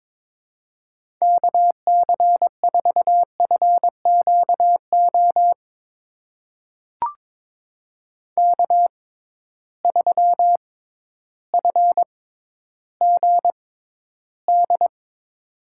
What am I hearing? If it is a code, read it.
Morse code, heard as a keyed tone, about 22 wpm, spaced as for 12 wpm, 700 Hz. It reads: KC4FQO K3FGD
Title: USA Call Signs - 6x Character Spacing - 22wpm